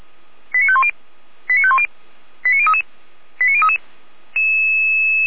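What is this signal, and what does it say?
Five-tone selective call (ZVEI), heard as quick runs of five tones, tones of about 70 ms. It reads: 89510 89510 89033 89033
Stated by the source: ZVEI five-tone selective call received on an SDR, recorded by the feuerwehr-orlamuende fire brigade.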